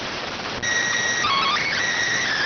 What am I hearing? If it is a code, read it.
SSTV (slow-scan television), PD 290.